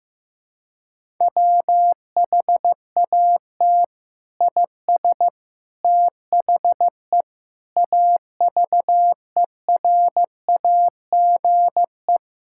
Morse code, heard as a keyed tone, about 15 wpm, 700 Hz. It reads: WHAT IS THE AVERAGE
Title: Sentences from Top 1000 Words - Review Part 6 - 15wpm